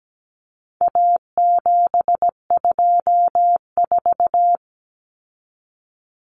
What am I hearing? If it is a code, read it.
Morse code, heard as a keyed tone, about 17 wpm, 700 Hz. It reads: A724
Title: Common Cut Numbers - 4-Digit 17wpm